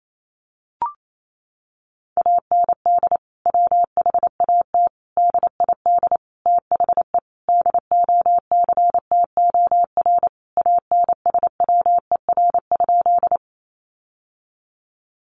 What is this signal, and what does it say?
Morse code, heard as a keyed tone, about 28 wpm, 700 Hz. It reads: AND WHAT DID THE DOCTOR ANSWER?
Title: Sentences from Top 1500 Words 28wpm